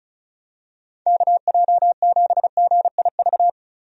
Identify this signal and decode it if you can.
Morse code, heard as a keyed tone, about 36 wpm, 700 Hz. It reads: KJ7GIV
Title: USA Call Signs - 6 Characters - Repeated 2x - 35wpm